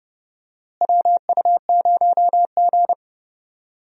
Morse code, heard as a keyed tone, about 30 wpm, 700 Hz. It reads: WU0G
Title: USA Call Signs - 4 Characters - Repeated 2x - 30wpm